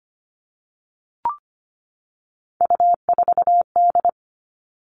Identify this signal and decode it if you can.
Morse code, heard as a keyed tone, about 25 wpm, 700 Hz. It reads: U4D
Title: All Cut Numbers - 3-Digit 25wpm